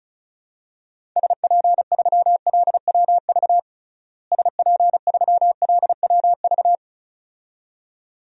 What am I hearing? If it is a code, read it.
Morse code, heard as a keyed tone, about 34 wpm, 700 Hz. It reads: SP3LWV SP3LWV
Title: DXCC-100 Call Signs - 6 Characters - Repeated 2x - 35wpm